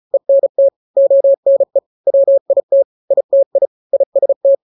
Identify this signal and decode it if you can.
Morse code, heard as a keyed tone, about 34 wpm, 550 Hz. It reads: ENT ONE WIT ITI IST
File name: 3-grams 35 30 WPM